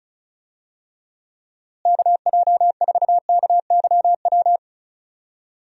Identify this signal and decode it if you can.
Morse code, heard as a keyed tone, about 36 wpm, 700 Hz. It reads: KJ4KYW
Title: USA Call Signs - 6 Characters - Repeated 3x - 35wpm